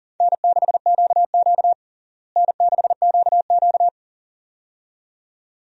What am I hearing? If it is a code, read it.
Morse code, heard as a keyed tone, about 40 wpm, 700 Hz. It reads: N6QQ N6QQ
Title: USA Call Signs - 4 Characters - Repeated 2x - 40wpm